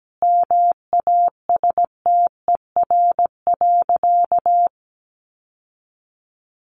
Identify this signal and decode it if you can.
Morse code, heard as a keyed tone, about 17 wpm, 700 Hz. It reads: MASTER.
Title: Sentences from Top 1300 Words 17wpm